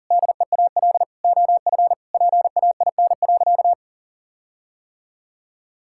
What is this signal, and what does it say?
Morse code, heard as a keyed tone, about 40 wpm, 700 Hz. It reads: DEAL OF PAIN.
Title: Sentences from Top 1600 Words 40wpm